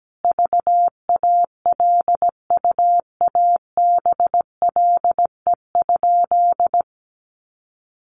Morse code, heard as a keyed tone, about 17 wpm, 700 Hz. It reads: VALUABLE?
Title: Sentences from Top 1100 Words 17wpm